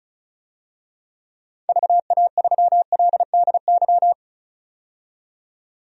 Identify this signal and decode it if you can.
Morse code, heard as a keyed tone, about 35 wpm, 700 Hz. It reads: VA3LDY
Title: DXCC-100 Call Signs - 6 Characters - 35wpm